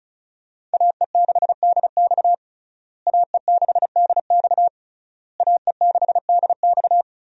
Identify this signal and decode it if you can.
Morse code, heard as a keyed tone, about 36 wpm, 700 Hz. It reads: AE6DX AE6DX AE6DX
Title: USA Call Signs - 5 Characters - Repeated 3x - 35wpm